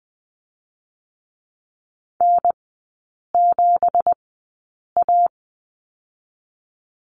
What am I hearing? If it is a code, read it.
Morse code, heard as a keyed tone, about 20 wpm, 700 Hz. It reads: N 7 A
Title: 198 Lesson 32e - Letter k - Three Characters 20wpm